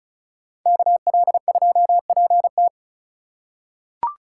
Morse code, heard as a keyed tone, about 36 wpm, 700 Hz. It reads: KL2PT
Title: USA Call Signs - 5 Characters - 3x Character Spacing - 35wpm